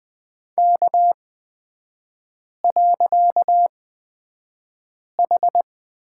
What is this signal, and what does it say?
Morse code, heard as a keyed tone, about 20 wpm, spaced as for 5 wpm, 700 Hz. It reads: K.H
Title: 255 Lesson 41e - Period - Three Characters 20wpm